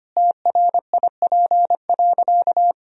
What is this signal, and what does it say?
Morse code, heard as a keyed tone, about 25 wpm, 700 Hz. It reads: TRIP.
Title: Sentences from Top 1000 Words - Review Part 4 - 25wpm